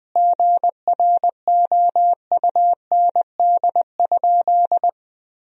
Morse code, heard as a keyed tone, about 20 wpm, 700 Hz. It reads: GROUND?
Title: Sentences from Top 1000 Words - Review Part 2 - 20wpm